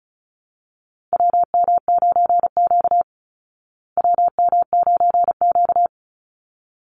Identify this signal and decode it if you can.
Morse code, heard as a keyed tone, about 34 wpm, 700 Hz. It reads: WM9Q WM9Q
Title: Call Signs Repeated 3x - 35wpm